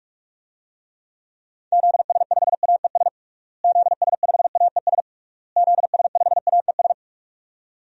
Morse code, heard as a keyed tone, about 46 wpm, 700 Hz. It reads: ZS5AES ZS5AES ZS5AES
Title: DXCC-100 Call Signs - 6 Characters - Repeated 3x - 45wpm